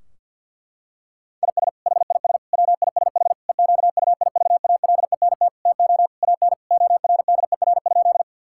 Morse code, heard as a keyed tone, about 50 wpm, 700 Hz. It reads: IS HIS WISH EQUIVALENT TO AN ORDER?
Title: Sentences from Top 1600 Words 50wpm